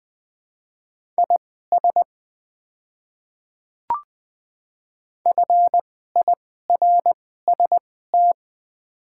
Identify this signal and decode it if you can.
Morse code, heard as a keyed tone, about 20 wpm, spaced as for 14 wpm, 700 Hz. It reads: IS FIRST
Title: 2x Character Spacing - Top 100 Words - 20wpm